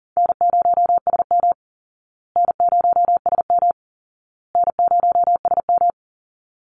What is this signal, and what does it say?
Morse code, heard as a keyed tone, about 37 wpm, 700 Hz. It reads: N0SM N0SM N0SM
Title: Easy Call Signs Repeated 3x - 40wpm